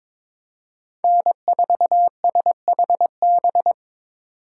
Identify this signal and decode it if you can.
Morse code, heard as a keyed tone, about 22 wpm, 700 Hz. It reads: N4SHB
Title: USA Call Signs - 5 Characters - 3x Character Spacing - 22wpm